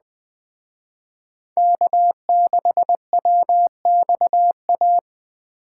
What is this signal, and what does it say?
Morse code, heard as a keyed tone, about 19 wpm, 700 Hz. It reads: K6WXA